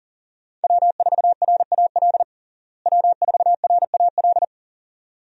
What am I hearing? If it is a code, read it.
Morse code, heard as a keyed tone, about 40 wpm, 700 Hz. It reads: W4RAL W4RAL
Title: USA Call Signs - 5 Characters - Repeated 2x - 40wpm